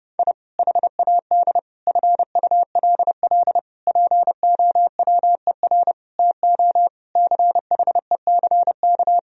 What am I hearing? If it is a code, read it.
Morse code, heard as a keyed tone, about 32 wpm, 700 Hz. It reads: I HAD FULL POWER TO CHECK
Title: Sentences from Top 1000 Words - Review Part 3 - 30wpm